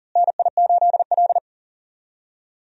Morse code, heard as a keyed tone, about 40 wpm, 700 Hz. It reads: NI8L
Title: USA Call Signs - 4 Characters - Repeated 3x - 40wpm